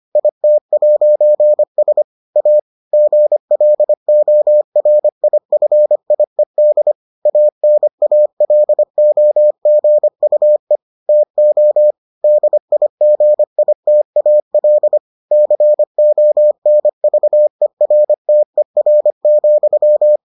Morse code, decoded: IT'S A GLORIFIED ANALOGUE TO DIGITAL CONVERTER,

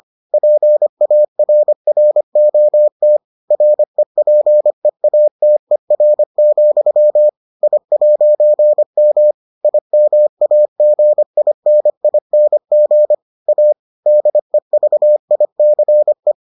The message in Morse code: PARROT REPEATER, I'M IMAGINING A DEVICE

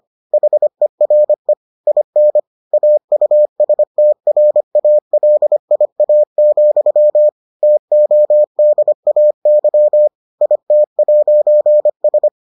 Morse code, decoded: HERE IN AUSTRALIA, TODAY IT'S

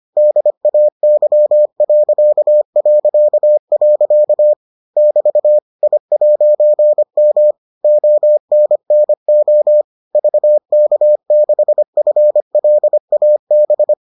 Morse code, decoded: DAY... <BT> I'M ONNO VK6FLAB

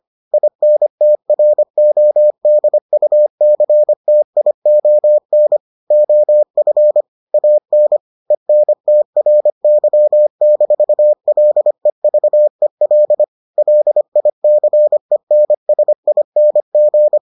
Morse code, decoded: INTRODUCTION OF AN ENTRY-LEVEL LICENSING

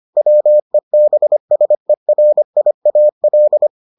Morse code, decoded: WEBSERIAL

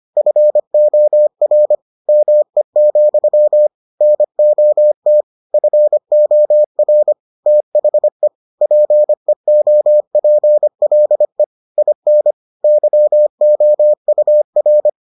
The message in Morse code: FOR ME, NOT FOR THE PEOPLE IN YOUR

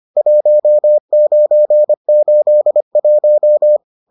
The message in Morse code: 1981

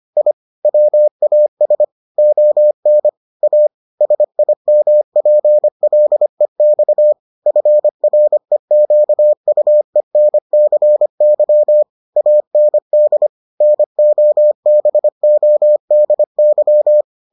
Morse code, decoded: I WAS ON A SIMPLEX FREQUENCY AND NOBODY

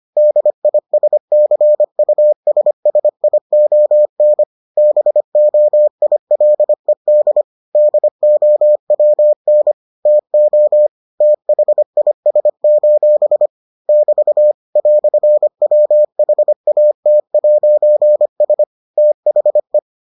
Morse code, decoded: DISCUSSION BOILED DOWN TO THIS: <BT> "WHAT'S THE

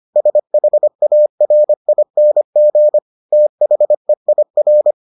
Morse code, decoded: SHARING THEIR

